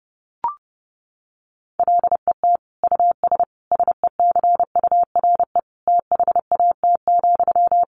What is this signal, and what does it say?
Morse code, heard as a keyed tone, about 30 wpm, 700 Hz. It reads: LET US SECURE THAT,
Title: Sentences from Top 1200 Words 30wpm